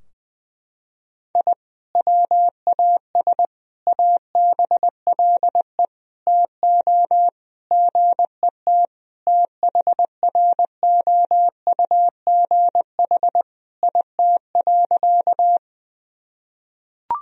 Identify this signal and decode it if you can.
Morse code, heard as a keyed tone, about 21 wpm, 700 Hz. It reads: I WAS ABLE TO GET THROUGH IT.
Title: Sentences from Top 1000 Words - Review Part 2 - 20wpm